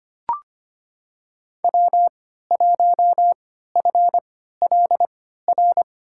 Morse code, heard as a keyed tone, about 25 wpm, spaced as for 14 wpm, 700 Hz. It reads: W1FLR
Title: USA Call Signs - 3x Character Spacing - 25wpm